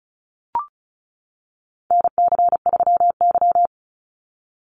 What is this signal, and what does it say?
Morse code, heard as a keyed tone, about 34 wpm, 700 Hz. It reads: NC3Y
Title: Call Signs - 35wpm